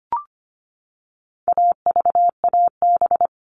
Morse code, heard as a keyed tone, about 25 wpm, 700 Hz. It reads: AVAB
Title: All Cut Numbers - 4-Digit 25wpm